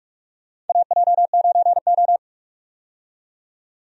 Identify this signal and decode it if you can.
Morse code, heard as a keyed tone, about 46 wpm, 700 Hz. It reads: AJ9O
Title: USA Call Signs - 4 Characters - 45wpm